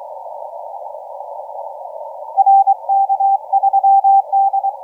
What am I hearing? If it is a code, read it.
Morse code, heard as a keyed tone, about 26 wpm, 780 Hz. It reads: RK3D
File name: FS K9EL 0329UTC 17m